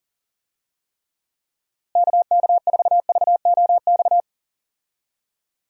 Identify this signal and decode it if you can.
Morse code, heard as a keyed tone, about 40 wpm, 700 Hz. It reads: KK4VOX
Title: USA Call Signs - 6 Characters - Repeated 2x - 40wpm